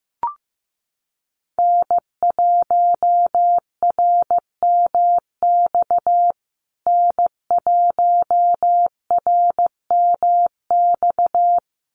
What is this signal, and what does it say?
Morse code, heard as a keyed tone, about 15 wpm, 700 Hz. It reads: N1RMX N1RMX
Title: USA Call Signs Repeated 2x 15wpm